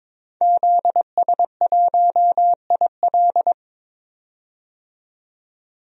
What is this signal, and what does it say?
Morse code, heard as a keyed tone, about 22 wpm, 700 Hz. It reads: ZS1IL
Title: DXCC-100 Call Signs - 5 Characters - 22wpm